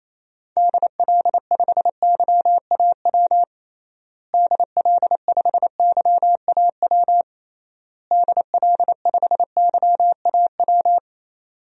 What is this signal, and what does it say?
Morse code, heard as a keyed tone, about 28 wpm, 700 Hz. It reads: DL5YAW DL5YAW DL5YAW
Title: DXCC-100 Call Signs - 6 Characters - Repeated 3x - 28wpm